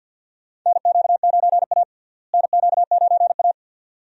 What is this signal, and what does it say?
Morse code, heard as a keyed tone, about 50 wpm, 700 Hz. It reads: NQ9A NQ9A
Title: USA Call Signs - 4 Characters - Repeated 2x - 50wpm